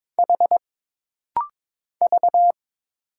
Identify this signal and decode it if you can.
Morse code, heard as a keyed tone, about 22 wpm, 700 Hz. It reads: H V
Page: This rapid-fire timing is geared to push your ICR (Instant Character Recognition) skills.